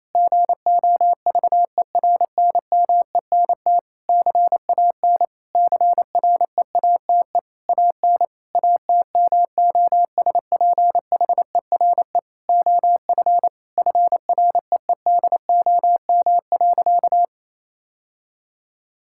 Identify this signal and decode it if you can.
Morse code, heard as a keyed tone, about 28 wpm, 700 Hz. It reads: GOVERNMENT CAN CREATE AN ATMOSPHERE OF FREEDOM.
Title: Sentences from Top 1600 Words 28wpm